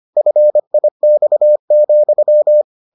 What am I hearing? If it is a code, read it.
Morse code, heard as a keyed tone, about 25 wpm, 600 Hz. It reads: FIX,